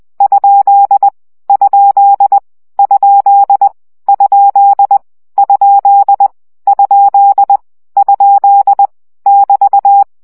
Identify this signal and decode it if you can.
Morse code, heard as a keyed tone, about 20 wpm, 800 Hz. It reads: ? ? ? ? ? ? ? <BT>
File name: CW gyakorlas kerdojel 17 baud